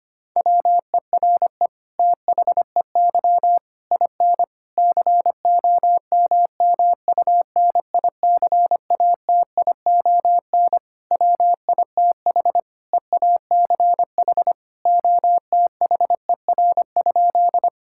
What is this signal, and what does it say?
Morse code, heard as a keyed tone, about 25 wpm, 700 Hz. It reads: WERE THEY IN COMMUNICATION WITH EACH OTHER?
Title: Sentences from Top 1000 Words - Review Part 6 - 25wpm